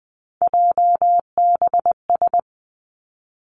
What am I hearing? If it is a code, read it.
Morse code, heard as a keyed tone, about 20 wpm, 700 Hz. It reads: JBS
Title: Three Challenging Letter-Number Combinations 20wpm